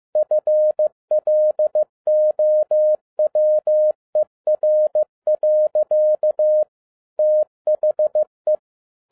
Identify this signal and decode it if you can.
Morse code, heard as a keyed tone, about 15 wpm, 610 Hz. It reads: FLOWER. THE
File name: 2012-01-mp3 / n07-ARA JR0DIL-15wpm